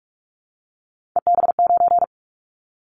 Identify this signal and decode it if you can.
Morse code, heard as a keyed tone, about 46 wpm, 700 Hz. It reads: EB9
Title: Callsign Prefixes for Top 100 DXCC - 45wpm